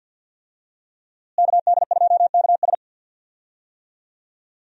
Morse code, heard as a keyed tone, about 48 wpm, 700 Hz. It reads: KD2KS
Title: USA Call Signs - 5 Characters - Repeated 3x - 50wpm